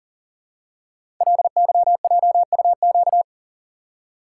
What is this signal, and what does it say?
Morse code, heard as a keyed tone, about 42 wpm, 700 Hz. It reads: LYJUQ
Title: Code Groups - 5 Letters - 40wpm